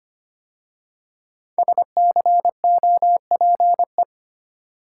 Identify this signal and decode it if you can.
Morse code, heard as a keyed tone, about 25 wpm, 700 Hz. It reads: SCOPE
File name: 5 Letter Words - First 3 Letters Given - 25wpm